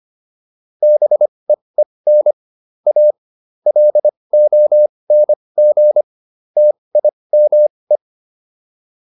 Morse code, heard as a keyed tone, about 25 wpm, spaced as for 20 wpm, 600 Hz. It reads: BEEN A LONG TIME